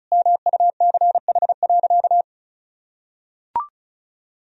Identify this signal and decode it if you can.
Morse code, heard as a keyed tone, about 36 wpm, 700 Hz. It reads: MUCH.
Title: Sentences from Top 1000 Words - Review Part 4 - 35wpm